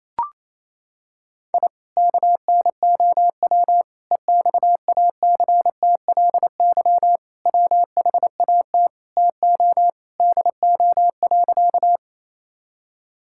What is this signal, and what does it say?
Morse code, heard as a keyed tone, about 28 wpm, 700 Hz. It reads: I KNOW EXACTLY WHAT TO DO.
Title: Sentences from Top 1000 Words - Review Part 6 - 28wpm